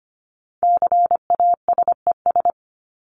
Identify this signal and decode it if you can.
Morse code, heard as a keyed tone, about 25 wpm, 700 Hz. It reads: CASES
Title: ES:S Suffix - Top 100 Words with Base Words 25wpm